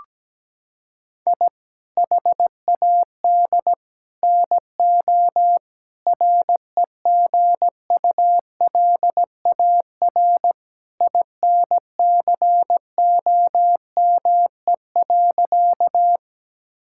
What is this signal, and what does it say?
Morse code, heard as a keyed tone, about 17 wpm, 700 Hz. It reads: I HAD NO REGULAR INCOME.
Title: Sentences from Top 1000 Words - Review Part 6 - 17wpm